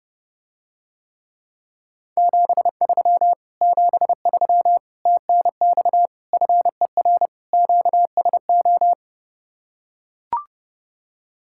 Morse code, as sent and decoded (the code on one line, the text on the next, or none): --... ...-- / --... ...-- / - -. -..- / ..-. .-. / --.- ... ---
73 73 TNX FER QSO